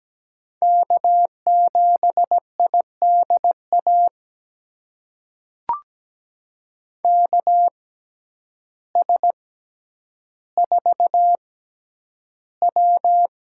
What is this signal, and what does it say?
Morse code, heard as a keyed tone, about 17 wpm, spaced as for 9 wpm, 700 Hz. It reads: K7IDA KS4W
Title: USA Call Signs - 5 Characters - 6x Character Spacing - 17wpm